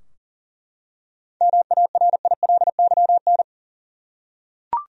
Morse code, heard as a keyed tone, about 40 wpm, 700 Hz. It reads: MARILYN
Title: USA Common Names - 40wpm